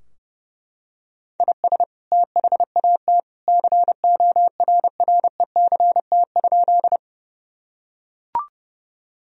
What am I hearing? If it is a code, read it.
Morse code, heard as a keyed tone, about 30 wpm, 700 Hz. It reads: IS THAT CORRECT?
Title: Sentences from Top 1000 Words - Review Part 2 - 30wpm